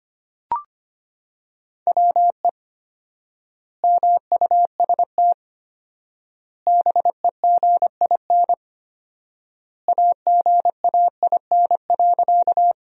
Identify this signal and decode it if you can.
Morse code, heard as a keyed tone, about 25 wpm, 700 Hz. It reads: WE MUST BEGIN AGAIN.
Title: 4x Word-Spacing - Sentences from Top 300 Words - 25wpm